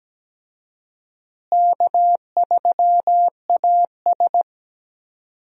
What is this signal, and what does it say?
Morse code, heard as a keyed tone, about 17 wpm, 700 Hz. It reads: K3AS
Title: USA Call Signs - 4 Characters - Repeated 2x - 17wpm